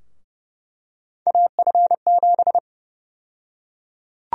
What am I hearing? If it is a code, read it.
Morse code, heard as a keyed tone, about 32 wpm, 700 Hz. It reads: AF7
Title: USA Call Sign Prefixes - Repeated 2x - 30wpm